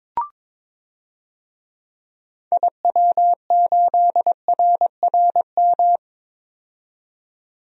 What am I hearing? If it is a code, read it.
Morse code, heard as a keyed tone, about 22 wpm, 700 Hz. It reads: IW8RRM
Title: DXCC-100 Call Signs - 6 Characters - 22wpm